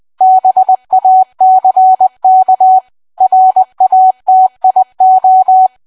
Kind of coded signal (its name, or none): Morse code